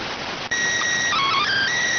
SSTV (slow-scan television), PD 290.